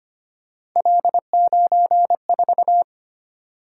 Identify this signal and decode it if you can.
Morse code, heard as a keyed tone, about 25 wpm, 700 Hz. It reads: L94
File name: Three Letter-Number Combinations - 25wpm